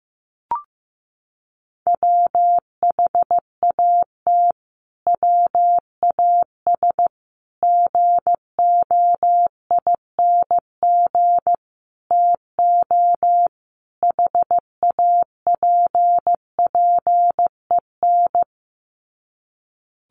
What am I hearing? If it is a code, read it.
Morse code, heard as a keyed tone, about 15 wpm, 700 Hz. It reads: WHAT WAS GOING TO HAPPEN
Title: Sets of 5 Words Encore 15wpm